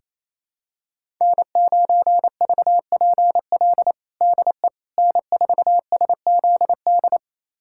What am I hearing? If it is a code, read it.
Morse code, heard as a keyed tone, about 28 wpm, 700 Hz. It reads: N9VPL DE N4SZD
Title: USA Callsign DE Callsign - 5 Characters - 28wpm